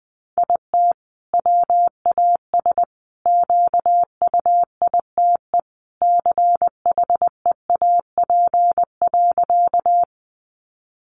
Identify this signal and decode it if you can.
Morse code, heard as a keyed tone, about 20 wpm, 700 Hz. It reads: IT WAS QUITE CHEAP.
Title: Sentences from Top 1200 Words 20wpm